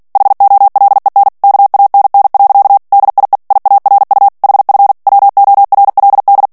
Morse code, heard as a keyed tone, about 48 wpm, 800 Hz. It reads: HOLEN KANN. DIE IARU HF WORLD
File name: morse mp3 / ALLEMAND texte clair 2 vitesse 45 mots minute